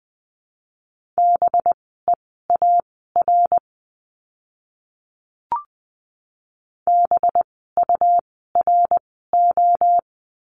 Morse code, heard as a keyed tone, about 20 wpm, spaced as for 14 wpm, 700 Hz. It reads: BEAR BURO
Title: 185 Lesson 30f - Letter b - Words 20wpm